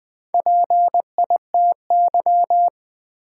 Morse code, heard as a keyed tone, about 20 wpm, 700 Hz. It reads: PITY
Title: Build A Word - 1 Letter At A Time - 3 to 5-Letter Words - 20wpm